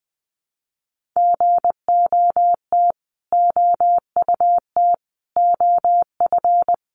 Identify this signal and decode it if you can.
Morse code, heard as a keyed tone, about 20 wpm, 700 Hz. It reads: GOT OUT OF
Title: Sets of 5 Words Encore 20wpm